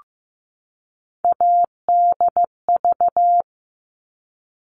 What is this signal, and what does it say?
Morse code, heard as a keyed tone, about 15 wpm, 700 Hz. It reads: ADV
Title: All Cut Numbers - 3-Digit 15wpm